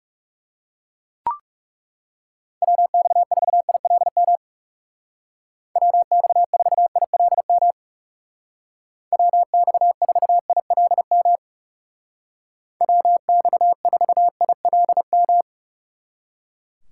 Morse code, heard as a keyed tone, about 36 wpm, 700 Hz. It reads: WX4ILM WX4ILM WX4ILM WX4ILM
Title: Speed Racing - USA Call Signs 30wpm